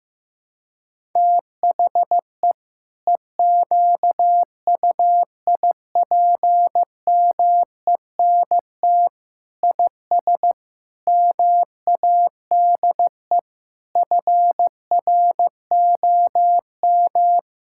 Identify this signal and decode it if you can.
Morse code, heard as a keyed tone, about 15 wpm, 700 Hz. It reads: THE EQUIPMENT IS MADE FROM
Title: Sentences from Top 1000 Words - Review Part 6 - 15wpm